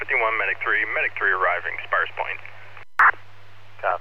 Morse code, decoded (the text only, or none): none